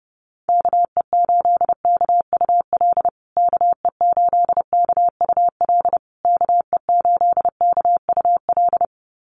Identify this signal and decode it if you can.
Morse code, heard as a keyed tone, about 28 wpm, 700 Hz. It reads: KE8KUL KE8KUL KE8KUL
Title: USA Call Signs Repeated 3x 30wpm